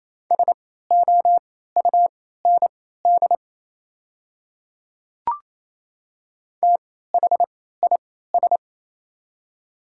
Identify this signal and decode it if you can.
Morse code, heard as a keyed tone, about 28 wpm, spaced as for 16 wpm, 700 Hz. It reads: SOUND THIS
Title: 3x Character Spacing - Top 200 Words - 28wpm